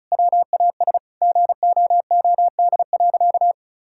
Morse code, heard as a keyed tone, about 36 wpm, 700 Hz. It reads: WAS GOOD.
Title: Sentences from Top 1000 Words - Review Part 6 - 35wpm